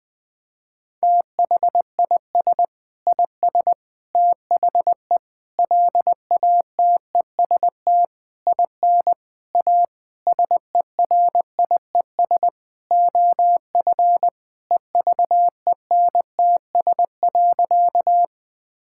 Morse code, heard as a keyed tone, about 20 wpm, 700 Hz. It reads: THIS IS THE LATEST IN A SERIES OF EVENTS.
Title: Sentences with 6-Word Phrases - 20wpm